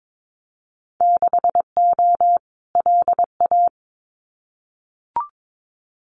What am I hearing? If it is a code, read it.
Morse code, heard as a keyed tone, about 22 wpm, 700 Hz. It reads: 6O LA
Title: Winter Field Day Exchanges Encore 22wpm